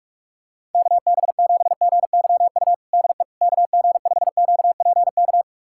Morse code, heard as a keyed tone, about 45 wpm, 700 Hz. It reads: KB7GYU DE KG5QPK
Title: USA Callsign DE Callsign - 6 Characters - 45wpm